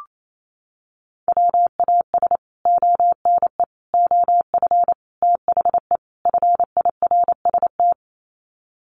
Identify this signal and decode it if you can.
Morse code, heard as a keyed tone, about 28 wpm, 700 Hz. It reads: WAS ONE OF THE FIRST
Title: Sets of 5 Words 28wpm